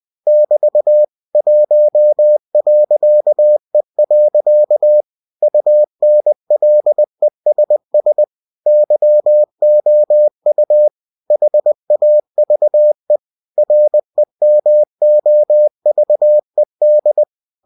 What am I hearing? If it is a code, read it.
Morse code, heard as a keyed tone, about 20 wpm, 600 Hz. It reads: <BT> 1.E. UNLESS YOU HAVE REMOVED
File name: cw-books / winnie-the-pooh / winnie-20wpm-page-0114